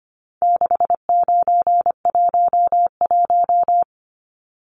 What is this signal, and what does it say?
Morse code, heard as a keyed tone, about 25 wpm, 700 Hz. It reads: 6911
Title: Numbers - 4-Digit 25wpm